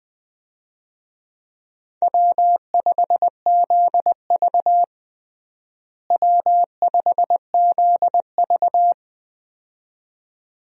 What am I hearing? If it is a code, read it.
Morse code, heard as a keyed tone, about 21 wpm, 700 Hz. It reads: W5ZV W5ZV
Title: USA Call Signs - 4 Characters - Repeated 2x - 20wpm